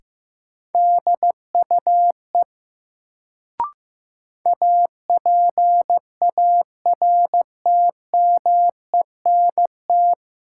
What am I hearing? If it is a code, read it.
Morse code, heard as a keyed tone, about 15 wpm, 700 Hz. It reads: DUE APARTMENT
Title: Sentences from Top 900 Words - New Word Drill - 15wpm